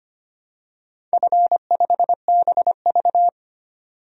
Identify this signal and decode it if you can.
Morse code, heard as a keyed tone, about 25 wpm, 700 Hz. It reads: F5BV
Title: DXCC-100 Call Signs - 4 Characters - 25wpm